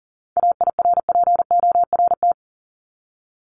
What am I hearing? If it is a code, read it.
Morse code, heard as a keyed tone, about 40 wpm, 700 Hz. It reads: AIRPORT
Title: Top 1751-2000 Words 40wpm